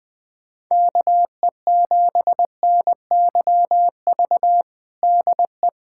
Morse code, heard as a keyed tone, about 21 wpm, 700 Hz. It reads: KE7NYV DE
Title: USA Callsign DE Callsign 20wpm